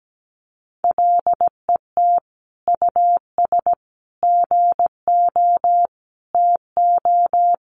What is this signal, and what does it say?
Morse code, heard as a keyed tone, about 17 wpm, 700 Hz. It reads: LET US GO TO
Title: Sentences from Top 1500 Words 17wpm